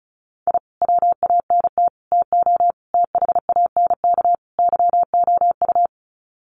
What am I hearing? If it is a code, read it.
Morse code, heard as a keyed tone, about 34 wpm, 700 Hz. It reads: I WANT TO THANK YOU
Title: Sets of 5 Words Encore 35wpm